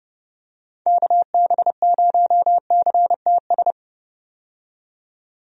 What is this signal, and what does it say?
Morse code, heard as a keyed tone, about 31 wpm, 700 Hz. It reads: KB0CTS
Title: USA Call Signs - 6 Characters - 30wpm